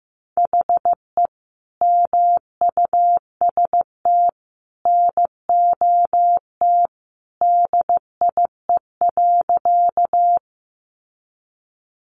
Morse code, heard as a keyed tone, about 15 wpm, 700 Hz. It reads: HE MUST NOT DIE.
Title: Sentences from Top 1300 Words 15wpm